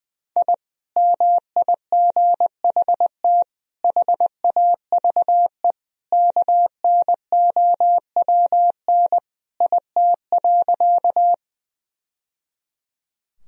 Morse code, heard as a keyed tone, about 20 wpm, 700 Hz. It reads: I MIGHT HAVE KNOWN IT.
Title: Sentences from Top 1000 Words - Review Part 3 - 20wpm